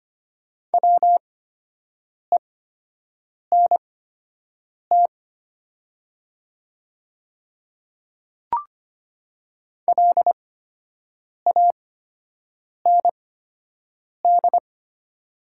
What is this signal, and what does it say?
Morse code, heard as a keyed tone, about 25 wpm, spaced as for 7 wpm, 700 Hz. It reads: WENT LAND
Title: 8x Character Spacing - Top 200 Words - 25wpm